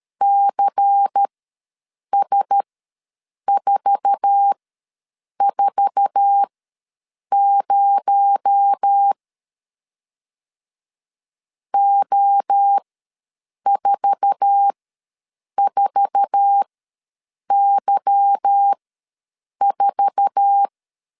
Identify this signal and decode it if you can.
Morse code, heard as a keyed tone, about 13 wpm, spaced as for 7 wpm, 800 Hz. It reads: CS440 O44Y4
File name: LEZIONI / 8 YJ45 / 8 YJ4 ws 12wpm 3249